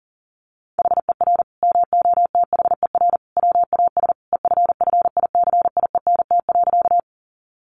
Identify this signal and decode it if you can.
Morse code, heard as a keyed tone, about 40 wpm, 700 Hz. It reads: HER MOTHER WAS EFFICIENT.
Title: Sentences from Top 1300 Words 40wpm